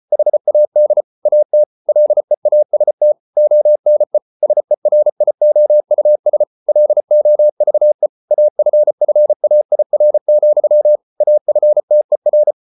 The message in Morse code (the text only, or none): HAD AT LEAST ONE SERIOUS LOVE AFFAIR, AFTER